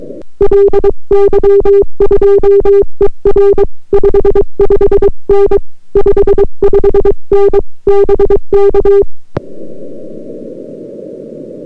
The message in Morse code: LY2ER 55N 55N BK